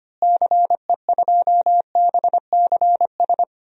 Morse code, heard as a keyed tone, about 25 wpm, 700 Hz. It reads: CE2BCS
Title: DXCC-100 Call Signs - 6 Characters - Repeated 2x - 25wpm